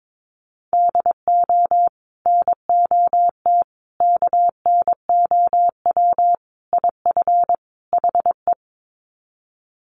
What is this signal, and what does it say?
Morse code, heard as a keyed tone, about 22 wpm, 700 Hz. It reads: DO NOT KNOW IF HE
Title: Sets of 5 Words Encore2 22wpm